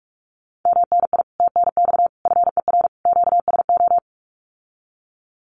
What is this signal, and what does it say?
Morse code, heard as a keyed tone, about 44 wpm, 700 Hz. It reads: MNI TNX FER QSO
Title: Common QSO Phrases Encore 45wpm